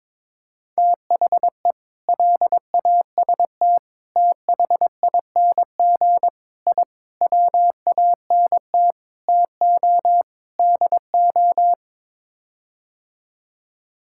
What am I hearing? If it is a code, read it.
Morse code, heard as a keyed tone, about 22 wpm, 700 Hz. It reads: THE LAST THING I WANT TO DO
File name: practice / 22wpm / Sets of 7 Words - 22wpm